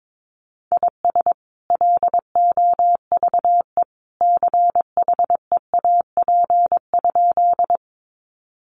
Morse code, heard as a keyed tone, about 22 wpm, 700 Hz. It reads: IS LOVE CHEAP?
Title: Sentences from Top 1200 Words 22wpm